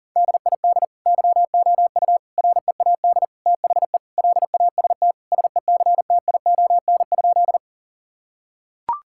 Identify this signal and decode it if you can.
Morse code, heard as a keyed tone, about 37 wpm, 700 Hz. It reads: DID YOU READ THE LAST SECTION?
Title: Sentences from Top 1000 Words - Review Part 5 - 40wpm